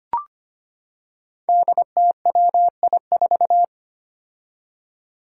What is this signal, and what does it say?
Morse code, heard as a keyed tone, about 25 wpm, 700 Hz. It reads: DTWI4
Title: Five Challenging Letter-Number Combinations 25wpm